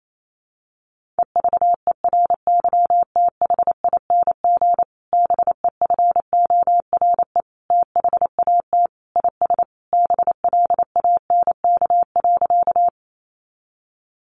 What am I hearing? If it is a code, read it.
Morse code, heard as a keyed tone, about 28 wpm, 700 Hz. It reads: EVERYTHING BEFORE THAT IS BLANK.
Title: Sentences from Top 1500 Words 28wpm